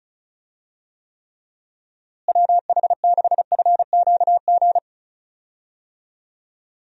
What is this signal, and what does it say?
Morse code, heard as a keyed tone, about 36 wpm, 700 Hz. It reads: WH6FQG
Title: USA Call Signs - 6 Characters - Repeated 3x - 35wpm